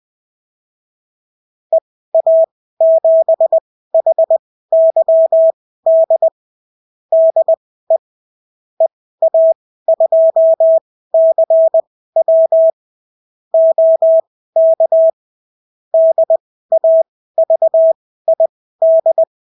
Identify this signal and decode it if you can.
Morse code, heard as a keyed tone, about 20 wpm, spaced as for 14 wpm, 650 Hz. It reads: EA7HYD DE EA2CW OK DAVID
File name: CW QSO libroCW4 20-10ppm